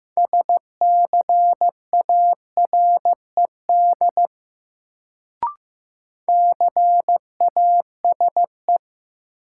Morse code, heard as a keyed tone, about 15 wpm, 700 Hz. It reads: SCARED CASE
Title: Sentences from Top 700 Words - New Word Drill - 15wpm